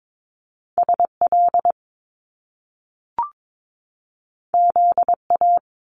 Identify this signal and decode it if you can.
Morse code, heard as a keyed tone, about 22 wpm, 700 Hz. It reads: SL ZA
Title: Top Two Letter Combinations in Words 22wpm